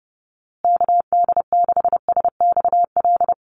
Morse code, heard as a keyed tone, about 29 wpm, 700 Hz. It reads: KD6SXL
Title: USA Call Signs Repeated 2x 30wpm